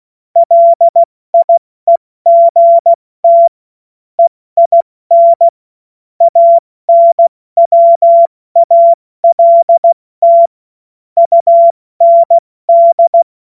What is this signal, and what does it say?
Morse code, heard as a keyed tone, about 16 wpm, 680 Hz. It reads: LIEGT EIN ANWALT UND
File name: Morsekurs / Witze / Witz 23. August 2022 70 F80